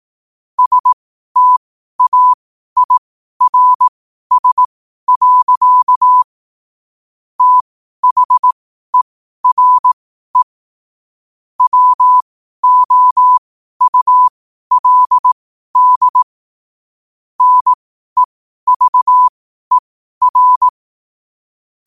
Morse code, heard as a keyed tone, about 18 wpm, spaced as for 12 wpm, 1000 Hz. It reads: STAIRS. THERE WOULD NEVER
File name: qotdfiles / 12wpm